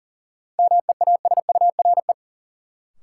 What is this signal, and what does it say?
Morse code, heard as a keyed tone, about 40 wpm, 700 Hz. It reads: MEASURE
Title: Sentences from Top 1000 Words - Review Part 1 - New Word Drill - 40wpm